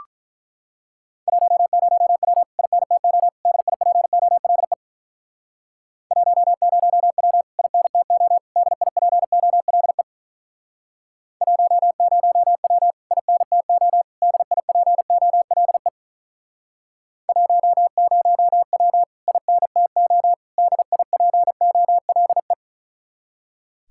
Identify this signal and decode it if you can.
Morse code, heard as a keyed tone, about 47 wpm, 700 Hz. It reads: 10W INTO DIPOLE 10W INTO DIPOLE 10W INTO DIPOLE 10W INTO DIPOLE